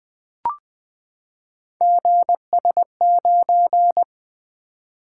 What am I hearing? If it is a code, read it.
Morse code, heard as a keyed tone, about 20 wpm, 700 Hz. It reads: GS9
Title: Three Letter-Number Combinations - 20wpm